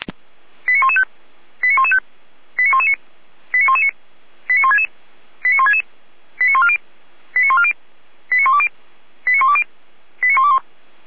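Five-tone selective call (ZVEI), heard as quick runs of five tones, tones of about 70 ms. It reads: 89115 89115 89119 89119 89166 89166 89140 89140 89120 89120 89121
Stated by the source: ZVEI five-tone call received via SDR, from the Feuerwehr Orlamünde audio server.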